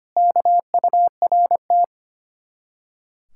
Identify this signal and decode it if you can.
Morse code, heard as a keyed tone, about 30 wpm, 700 Hz. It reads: KURT
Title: Speed-Racing Common QSO Phrases Encore - 25wpm